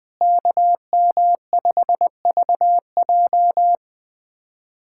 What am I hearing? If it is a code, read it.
Morse code, heard as a keyed tone, about 21 wpm, 700 Hz. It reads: KM5VJ
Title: USA Callsign DE Callsign - 5 Characters - 20wpm